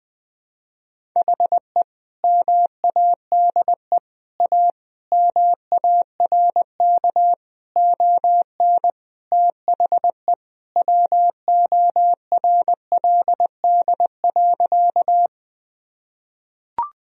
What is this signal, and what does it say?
Morse code, heard as a keyed tone, about 21 wpm, 700 Hz. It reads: HE MADE A MARK ON THE WORLD.
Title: Sentences from Top 1000 Words - Review Part 1 - 20wpm